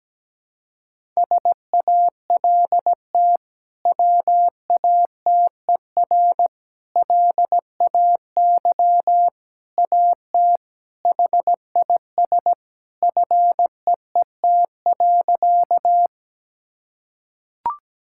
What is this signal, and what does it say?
Morse code, heard as a keyed tone, about 17 wpm, 700 Hz. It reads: SALT WATER LAY AT HIS FEET.
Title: Sentences from Top 1000 Words - Review Part 5 - 17wpm